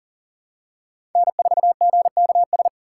Morse code, heard as a keyed tone, about 40 wpm, 700 Hz. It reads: N4GKS
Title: USA Call Signs - 4x Character Spacing - 40wpm